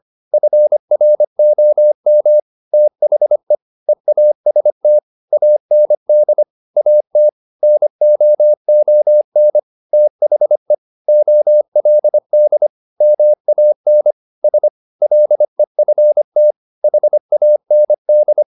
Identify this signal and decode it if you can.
Morse code, heard as a keyed tone, about 25 wpm, 600 Hz. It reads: FROM THE EAST AND AT NOON THE OLD MAN S LEFT HAND
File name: the old man and the sea part 1 25wpm